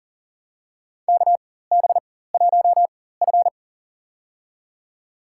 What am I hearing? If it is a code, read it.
Morse code, heard as a keyed tone, about 39 wpm, spaced as for 19 wpm, 700 Hz. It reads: KB1F
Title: USA Call Signs - 4 Characters - 4x Character Spacing - 40wpm